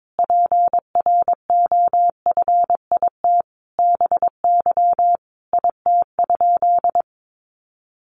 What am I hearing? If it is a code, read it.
Morse code, heard as a keyed tone, about 22 wpm, 700 Hz. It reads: PROFIT BY IT?
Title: Sentences from Top 800 Words 22wpm